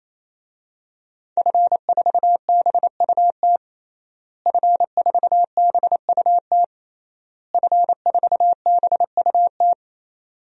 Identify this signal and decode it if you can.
Morse code, heard as a keyed tone, about 27 wpm, 700 Hz. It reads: F4BUT F4BUT F4BUT
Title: DXCC-100 Call Signs - 5 Characters - Repeated 3x - 28wpm